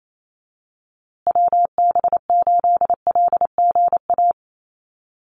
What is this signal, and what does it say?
Morse code, heard as a keyed tone, about 28 wpm, 700 Hz. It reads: WB8LGA
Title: USA Call Signs Repeated 2x 28wpm